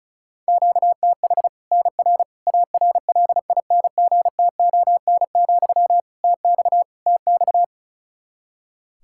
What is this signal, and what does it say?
Morse code, heard as a keyed tone, about 34 wpm, 700 Hz. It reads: QTH NR ARLINGTON, TX TX
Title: K1USN SST QSO Practice 35wpm